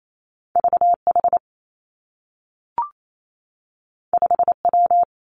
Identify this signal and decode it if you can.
Morse code, heard as a keyed tone, about 28 wpm, 700 Hz. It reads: V H 5 W
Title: Two Challenging Letter-Number Combinations 28wpm